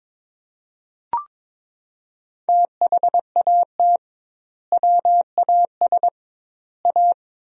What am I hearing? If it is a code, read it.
Morse code, heard as a keyed tone, about 22 wpm, 700 Hz. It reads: THAT WAS A
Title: 2x Word-Spacing - Sentences from Top 200 Words - 22wpm